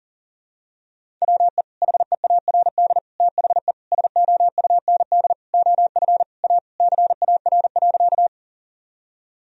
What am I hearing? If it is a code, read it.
Morse code, heard as a keyed tone, about 40 wpm, 700 Hz. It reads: WE HEARD THE SOUND OF A CAR.
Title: Sentences from Top 1000 Words - Review Part 1 - 40wpm